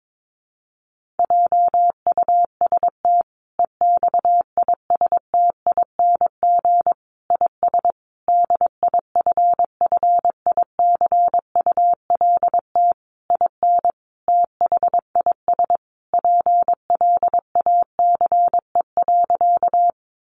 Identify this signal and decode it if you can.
Morse code, heard as a keyed tone, about 22 wpm, 700 Hz. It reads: JUST EXISTING IS DIFFICULT IN THIS PLACE.
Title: Sentences from Top 1500 Words 22wpm